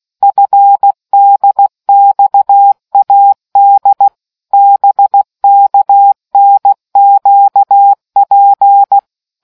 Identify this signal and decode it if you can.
Morse code, heard as a keyed tone, about 16 wpm, 800 Hz. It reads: FDXAD BKNQP